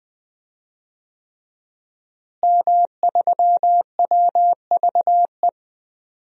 Morse code, heard as a keyed tone, about 20 wpm, 700 Hz. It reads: M3WVE
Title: DXCC-100 Call Signs - 5 Characters - Repeated 3x - 20wpm